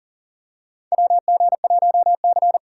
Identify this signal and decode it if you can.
Morse code, heard as a keyed tone, about 40 wpm, 690 Hz. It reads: WG1C
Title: USA Call Signs - 4 Characters - 2x Character Spacing - 40wpm